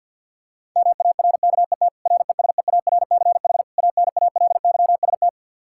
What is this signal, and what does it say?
Morse code, heard as a keyed tone, about 52 wpm, 700 Hz. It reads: MARKET RESEARCH ANALYST
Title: Practice QSO - Exchange 3 - More Info - 50wpm